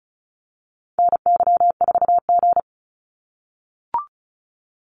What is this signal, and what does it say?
Morse code, heard as a keyed tone, about 36 wpm, 700 Hz. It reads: NY4G
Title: NA SOTA Activator Callsigns 35wpm